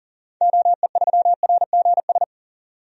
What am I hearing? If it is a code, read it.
Morse code, heard as a keyed tone, about 40 wpm, 700 Hz. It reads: OE3RGS
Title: DXCC-100 Call Signs - 6 Characters - 40wpm